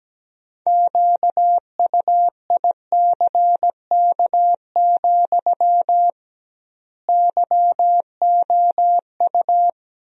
Morse code, heard as a keyed tone, about 17 wpm, 700 Hz. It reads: QUICK, YOU
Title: 2x Word-Spacing - Sentences from Top 500 Words - 17wpm